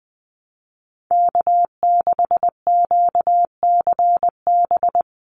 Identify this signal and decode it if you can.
Morse code, heard as a keyed tone, about 20 wpm, 700 Hz. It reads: K6QCB
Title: NA SOTA Chaser Callsigns 20wpm